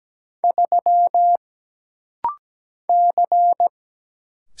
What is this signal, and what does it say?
Morse code, heard as a keyed tone, about 17 wpm, 700 Hz. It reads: 3 C
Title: Single Letter-Number Alt - Rapid-Fire 17wpm